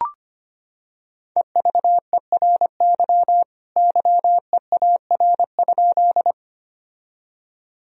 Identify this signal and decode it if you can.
Morse code, heard as a keyed tone, about 25 wpm, 700 Hz. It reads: EVERY YEAR?
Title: Sentences from Top 1000 Words - Review Part 1 - 25wpm